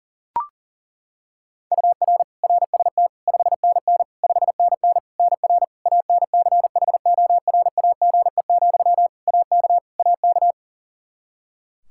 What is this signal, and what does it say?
Morse code, heard as a keyed tone, about 40 wpm, 700 Hz. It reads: UR RST 5NN 5NN NR ANCHORAGE, AK AK
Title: Practice QSO - Exchange 1 - Beginning - 40wpm